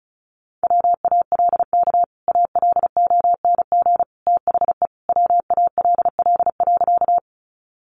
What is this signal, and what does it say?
Morse code, heard as a keyed tone, about 34 wpm, 700 Hz. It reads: WALK ALONG THE WALL.
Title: Sentences from Top 1100 Words 35wpm